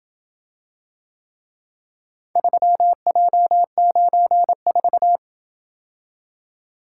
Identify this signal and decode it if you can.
Morse code, heard as a keyed tone, about 27 wpm, 700 Hz. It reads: 3J94